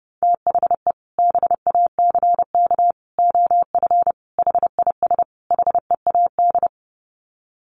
Morse code, heard as a keyed tone, about 30 wpm, 700 Hz. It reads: THE BACK OF HIS HEAD
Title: Sets of 5 Words Encore 30wpm